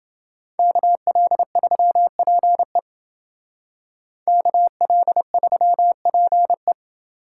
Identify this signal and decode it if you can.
Morse code, heard as a keyed tone, about 28 wpm, 700 Hz. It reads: KL3PE KL3PE